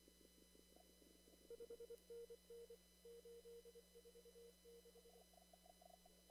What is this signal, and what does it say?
Morse code, heard as a keyed tone, about 23 wpm, 470 Hz. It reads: 5NN 846
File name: CW / kt5j wpx cw-op-k5pi